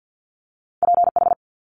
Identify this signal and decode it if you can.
Morse code, heard as a keyed tone, about 50 wpm, 700 Hz. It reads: PH
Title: Multi-letter Phonemes Quick Fire 50wpm